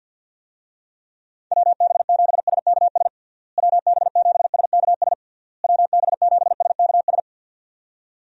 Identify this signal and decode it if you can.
Morse code, heard as a keyed tone, about 48 wpm, 700 Hz. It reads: WB7SKS WB7SKS WB7SKS
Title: USA Call Signs - 6 Characters - Repeated 3x - 50wpm